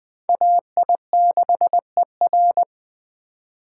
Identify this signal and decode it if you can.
Morse code, heard as a keyed tone, about 19 wpm, 700 Hz. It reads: AI6ER